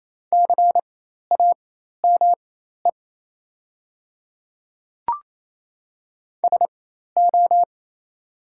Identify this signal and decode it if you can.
Morse code, heard as a keyed tone, about 28 wpm, spaced as for 13 wpm, 700 Hz. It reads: CAME SO
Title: 4x Character Spacing - Top 200 Words - 28wpm